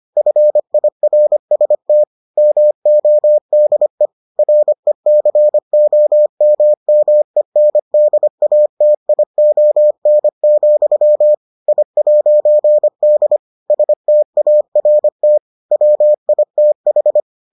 Morse code, heard as a keyed tone, about 25 wpm, 600 Hz. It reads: FIRST MODE RECOMMENDATION, I'D START WITH